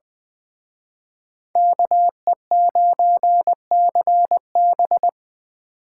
Morse code, heard as a keyed tone, about 24 wpm, 700 Hz. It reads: KE9CB
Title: Speed Racing - USA Call Signs 20wpm